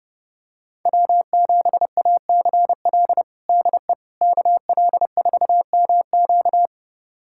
Morse code, heard as a keyed tone, about 30 wpm, 700 Hz. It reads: W7ACL DE KL4MQ
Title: USA Callsign DE Callsign - 5 Characters - 30wpm